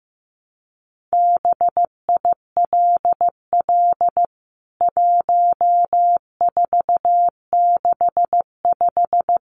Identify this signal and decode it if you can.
Morse code, heard as a keyed tone, about 15 wpm, 700 Hz. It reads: BILL 1465
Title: CWT Exchange 15wpm